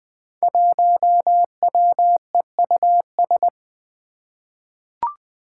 Morse code, heard as a keyed tone, about 20 wpm, 700 Hz. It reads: 1WEUS
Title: Five Challenging Letter-Number Combinations 20wpm